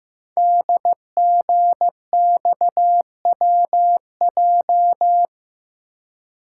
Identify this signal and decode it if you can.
Morse code, heard as a keyed tone, about 15 wpm, 700 Hz. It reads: DGXWJ
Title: Code Groups - 5 Letters - 15wpm